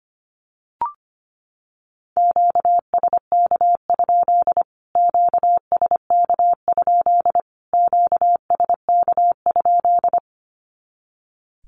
Morse code, heard as a keyed tone, about 25 wpm, 700 Hz. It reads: QSK? QSK? QSK?
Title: Q-Codes Repeated 3x 25wpm